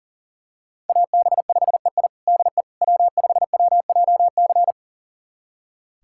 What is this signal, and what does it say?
Morse code, heard as a keyed tone, about 40 wpm, 700 Hz. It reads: AB5EI DE W5WJC
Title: USA Callsign DE Callsign - 5 Characters - 40wpm